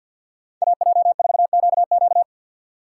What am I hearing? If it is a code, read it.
Morse code, heard as a keyed tone, about 58 wpm, 700 Hz. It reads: AJ4QQ
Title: Speed Racing - USA 2x2 Call Signs 50wpm